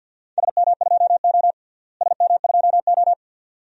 Morse code, heard as a keyed tone, about 51 wpm, 700 Hz. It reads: SM2O SM2O
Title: DXCC-100 Call Signs - 4 Characters - Repeated 2x - 50wpm